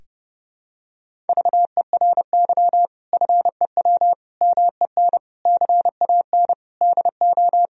Morse code, heard as a keyed tone, about 30 wpm, 700 Hz. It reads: VERY FEW MEN CAN DO
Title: Sentences from Top 1000 Words - Review Part 1 - 30wpm